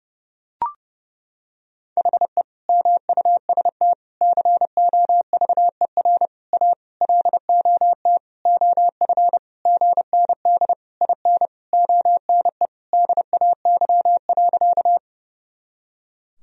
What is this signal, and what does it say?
Morse code, heard as a keyed tone, about 30 wpm, 700 Hz. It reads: HE MUST COVER A LOT OF GND IN ONE DAY.
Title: Sentences from Top 500 Words and Abbreviations 30wpm